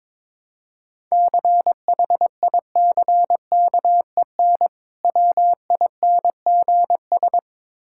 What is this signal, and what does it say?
Morse code, heard as a keyed tone, about 23 wpm, 690 Hz. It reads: CHICKEN WINGS
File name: Difficult Words with Hints - 22wpm